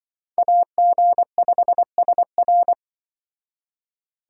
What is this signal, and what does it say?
Morse code, heard as a keyed tone, about 24 wpm, 700 Hz. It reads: AG5SR
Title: Speed Racing - USA Call Signs 20wpm